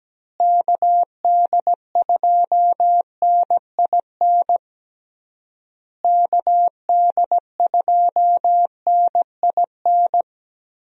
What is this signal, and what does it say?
Morse code, heard as a keyed tone, about 17 wpm, 700 Hz. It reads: KD2NIN KD2NIN
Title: USA Call Signs - 6 Characters - Repeated 3x - 17wpm